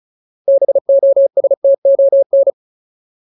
Morse code, heard as a keyed tone, about 34 wpm, 550 Hz. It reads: BOSTON
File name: Ultimate Word List / 35WPM Ultimate